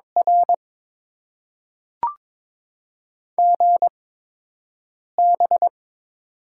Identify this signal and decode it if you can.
Morse code, heard as a keyed tone, about 22 wpm, spaced as for 6 wpm, 700 Hz. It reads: R GB
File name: Top 90 QSO Elements - 8x Character Spacing - 22wpm